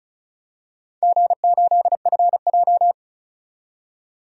Morse code, heard as a keyed tone, about 34 wpm, 700 Hz. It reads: G8FJ
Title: DXCC-100 Call Signs - 4 Characters - 35wpm